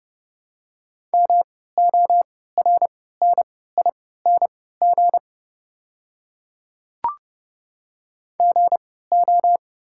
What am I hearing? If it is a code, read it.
Morse code, heard as a keyed tone, about 30 wpm, spaced as for 17 wpm, 690 Hz. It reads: MORNING GO
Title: 3x Character Spacing - Top 500 Words - 30wpm